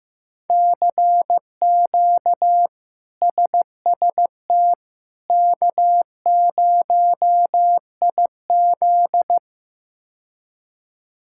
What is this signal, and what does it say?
Morse code, heard as a keyed tone, about 15 wpm, 700 Hz. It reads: CQ SST K0IZ
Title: K1USN SST QSO Practice - 15wpm